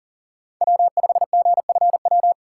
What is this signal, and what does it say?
Morse code, heard as a keyed tone, about 40 wpm, 700 Hz. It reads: W5GFW
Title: USA Call Signs - 5 Characters - 4x Character Spacing - 40wpm